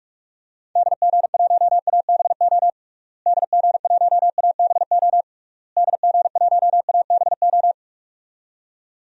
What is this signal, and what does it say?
Morse code, heard as a keyed tone, about 41 wpm, 700 Hz. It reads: DG1ABO DG1ABO DG1ABO
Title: DXCC-100 Call Signs - 6 Characters - Repeated 3x - 45wpm